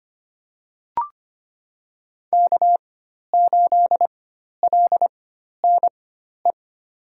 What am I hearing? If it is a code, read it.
Morse code, heard as a keyed tone, about 25 wpm, spaced as for 12 wpm, 700 Hz. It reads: K8LNE
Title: USA Call Signs - 5 Characters - 4x Character Spacing - 25wpm